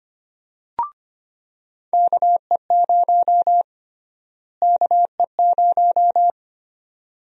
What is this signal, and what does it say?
Morse code, heard as a keyed tone, about 25 wpm, 700 Hz. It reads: KE0 KE0
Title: USA Call Sign Prefixes - Repeated 2x - 25wpm